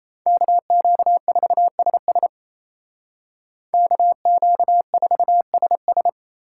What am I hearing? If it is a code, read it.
Morse code, heard as a keyed tone, about 30 wpm, 700 Hz. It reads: KQ4SS KQ4SS
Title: Speed Racing - USA 2x2 Call Signs 28wpm